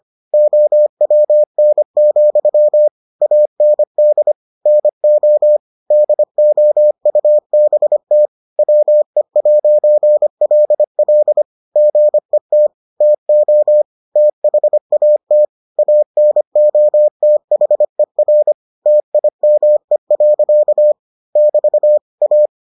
Morse code, heard as a keyed tone, about 25 wpm, 600 Hz. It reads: OWN, AND NO DOUBT WE'LL GET TO THAT ANOTHER TIME. <BT> A